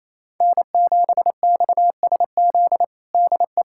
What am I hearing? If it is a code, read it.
Morse code, heard as a keyed tone, about 28 wpm, 700 Hz. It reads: N7XSZ DE
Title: USA Callsign DE Callsign - 5 Characters - 28wpm